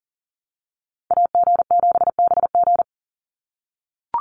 Morse code, heard as a keyed tone, about 40 wpm, 700 Hz. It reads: AG7BG
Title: USA Call Signs 40wpm